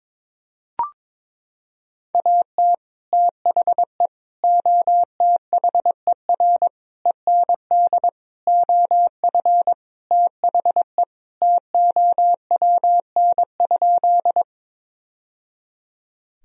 Morse code, heard as a keyed tone, about 22 wpm, 700 Hz. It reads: AT THE OTHER END OF THE TOWN?
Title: Sentences from Top 1000 Words - Review Part 2 - 22wpm